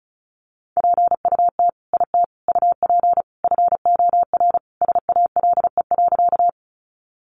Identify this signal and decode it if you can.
Morse code, heard as a keyed tone, about 34 wpm, 700 Hz. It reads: PUT IT UP FOR SALE.
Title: Sentences from Top 1400 Words 35wpm